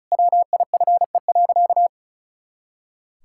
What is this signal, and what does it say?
Morse code, heard as a keyed tone, about 34 wpm, 700 Hz. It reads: WIFE.
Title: Pangram Challenge v2 - 35wpm